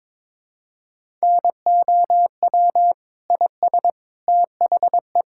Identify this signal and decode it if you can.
Morse code, heard as a keyed tone, about 22 wpm, 700 Hz. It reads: NOW IS THE
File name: practice / 22wpm / Sentences with 2-Word Phrases - 22wpm